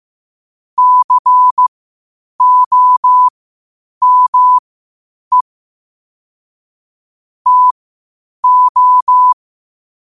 Morse code, heard as a keyed tone, about 15 wpm, spaced as for 8 wpm, 1000 Hz. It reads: COME TO
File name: qotdfiles / 8wpm